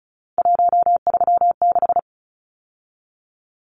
Morse code, heard as a keyed tone, about 34 wpm, 700 Hz. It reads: 136
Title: Numbers - 3-Digit 35wpm